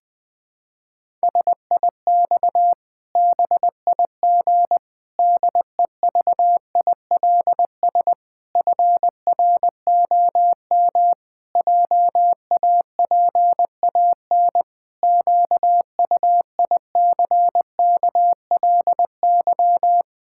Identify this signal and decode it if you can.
Morse code, heard as a keyed tone, about 20 wpm, 700 Hz. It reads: SIX BIG DEVILS FROM JAPAN QUICKLY
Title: Pangram Challenge v2 - 20wpm